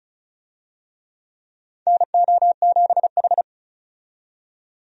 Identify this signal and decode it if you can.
Morse code, heard as a keyed tone, about 34 wpm, 700 Hz. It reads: NO7H
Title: USA Call Signs - 4 Characters - 35wpm